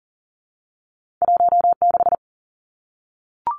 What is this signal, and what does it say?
Morse code, heard as a keyed tone, about 40 wpm, 700 Hz. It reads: 16
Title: Numbers - 2-Digit 40wpm